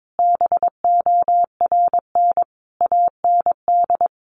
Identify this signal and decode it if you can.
Morse code, heard as a keyed tone, about 22 wpm, 700 Hz. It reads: BORN AND
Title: Sentences from Top 1300 Words 22wpm